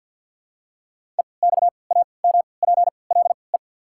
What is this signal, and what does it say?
Morse code, heard as a keyed tone, about 53 wpm, spaced as for 29 wpm, 700 Hz. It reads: EXAMPLE
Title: 3x Character Spacing - Top 500 Words - 50wpm